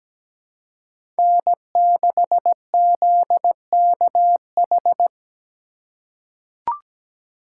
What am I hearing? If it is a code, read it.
Morse code, heard as a keyed tone, about 17 wpm, 700 Hz. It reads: N6ZKH
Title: USA Call Signs - 5 Characters - 2x Character Spacing - 17wpm